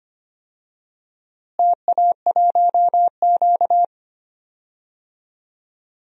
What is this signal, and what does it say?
Morse code, heard as a keyed tone, about 25 wpm, 700 Hz. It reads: TA1Q
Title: DXCC-100 Call Signs - 4 Characters - Repeated 2x - 25wpm